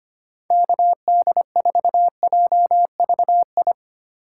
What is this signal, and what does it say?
Morse code, heard as a keyed tone, about 25 wpm, 700 Hz. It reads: KD4JVI
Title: USA Call Signs - 3x Character Spacing - 25wpm